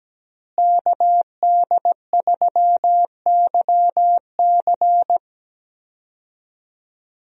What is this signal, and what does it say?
Morse code, heard as a keyed tone, about 17 wpm, 700 Hz. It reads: KD3YC
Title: USA Call Signs - 5 Characters - Repeated 3x - 17wpm